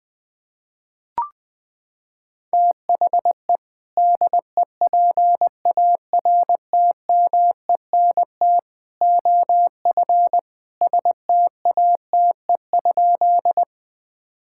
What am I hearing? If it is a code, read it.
Morse code, heard as a keyed tone, about 20 wpm, 700 Hz. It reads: THE DEPARTMENT OF STATE?
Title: Sentences from Top 1000 Words - Review Part 5 - 20wpm